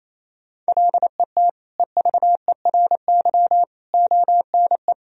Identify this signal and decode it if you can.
Morse code, heard as a keyed tone, about 28 wpm, 700 Hz. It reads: LET EVERY ONE
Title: Sentences from Top 1000 Words - Review Part 1 - 28wpm